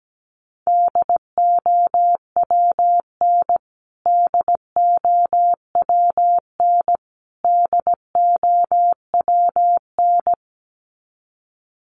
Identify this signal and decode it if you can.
Morse code, heard as a keyed tone, about 17 wpm, 700 Hz. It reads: DOWN DOWN DOWN
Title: Top 300 Words Repeated 17wpm